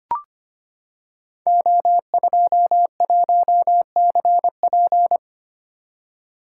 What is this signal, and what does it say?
Morse code, heard as a keyed tone, about 25 wpm, 700 Hz. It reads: O21CP
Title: Code Groups - 5 Letter:Numbers - 25wpm